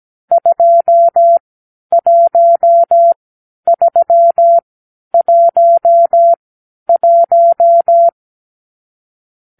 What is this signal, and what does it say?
Morse code, heard as a keyed tone, about 17 wpm, spaced as for 10 wpm, 680 Hz. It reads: 21311
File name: k7qo / Lesson 44 (1-3)